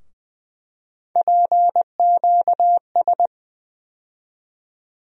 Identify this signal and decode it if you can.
Morse code, heard as a keyed tone, about 20 wpm, 700 Hz. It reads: PQS
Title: USA Call Sign Postfixes - 3 Letters 20wpm